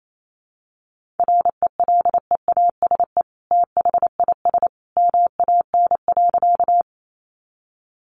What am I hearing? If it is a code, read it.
Morse code, heard as a keyed tone, about 28 wpm, 700 Hz. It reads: RELEASE THIS MAN.
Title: Sentences from Top 1100 Words 28wpm